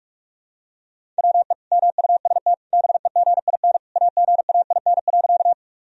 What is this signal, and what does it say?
Morse code, heard as a keyed tone, about 46 wpm, 700 Hz. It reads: WE MUST BEGIN AGAIN.
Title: Sentences from Top 1000 Words - Review Part 1 - 45wpm